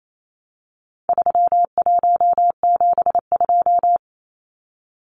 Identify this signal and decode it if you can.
Morse code, heard as a keyed tone, about 28 wpm, 700 Hz. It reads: 3172
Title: Numbers - 4-Digit 28wpm